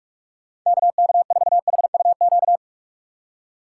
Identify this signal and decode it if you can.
Morse code, heard as a keyed tone, about 46 wpm, 700 Hz. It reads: KK4HUQ
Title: USA Call Signs - 6 Characters - 2x Character Spacing - 45wpm